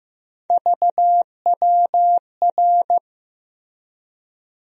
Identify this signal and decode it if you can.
Morse code, heard as a keyed tone, about 15 wpm, 700 Hz. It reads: VWR
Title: USA Call Sign Postfixes - 3 Letters 15wpm